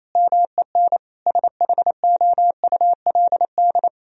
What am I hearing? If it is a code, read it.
Morse code, heard as a keyed tone, about 28 wpm, 700 Hz. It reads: MEN SHOULD
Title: Sentences from Top 1000 Words - Review Part 1 - 28wpm